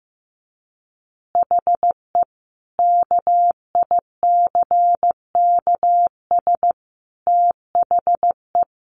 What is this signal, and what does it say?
Morse code, heard as a keyed tone, about 14 wpm, 700 Hz. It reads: HE KICKS THE
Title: Additional Easy Phrases 15wpm